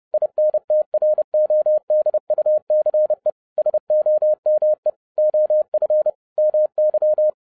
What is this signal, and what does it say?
Morse code, heard as a keyed tone, about 30 wpm, 610 Hz. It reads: INTRODUCE SOME OF MY